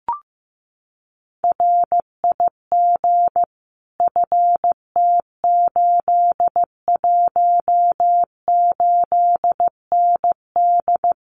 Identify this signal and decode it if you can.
Morse code, heard as a keyed tone, about 15 wpm, 700 Hz. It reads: RIG FT818ND
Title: Common QSO Phrases Encore 15wpm